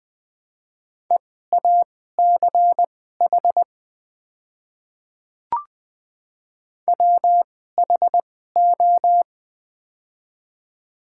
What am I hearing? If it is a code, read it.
Morse code, heard as a keyed tone, about 20 wpm, spaced as for 14 wpm, 700 Hz. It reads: EACH WHO
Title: 2x Character Spacing - Top 100 Words - 20wpm